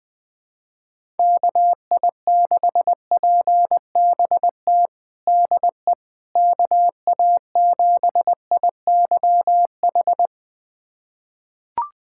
Morse code, heard as a keyed tone, about 20 wpm, 700 Hz. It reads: KI6PBT DE KA7IYH
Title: USA Callsign DE Callsign - 6 Characters - 20wpm